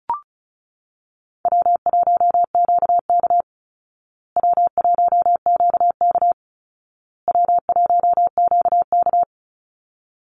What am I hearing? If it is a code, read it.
Morse code, heard as a keyed tone, about 34 wpm, 700 Hz. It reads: W1QK W1QK W1QK
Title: Easy Call Signs Repeated 3x - 35wpm